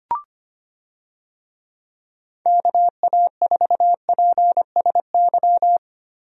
Morse code, heard as a keyed tone, about 25 wpm, 700 Hz. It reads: KA4PSY
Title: USA Call Signs - 6 Characters - Repeated 2x - 25wpm